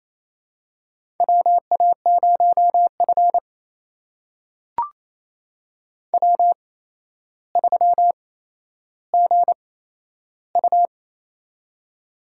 Morse code, heard as a keyed tone, about 28 wpm, spaced as for 12 wpm, 700 Hz. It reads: WA0F W3GU
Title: USA Call Signs - 4 Characters - 8x Character Spacing - 28wpm